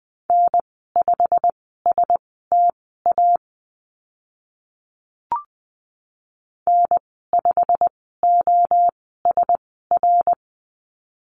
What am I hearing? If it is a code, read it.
Morse code, heard as a keyed tone, about 20 wpm, spaced as for 14 wpm, 700 Hz. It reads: N5STA N5OSR
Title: 088 Lesson 15g - Number 5 - Call Signs 20wpm